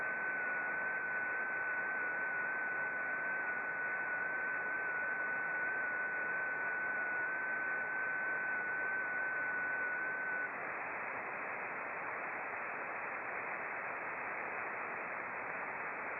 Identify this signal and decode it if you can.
FT8.
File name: contests / sept-vhf-k5tr- -0300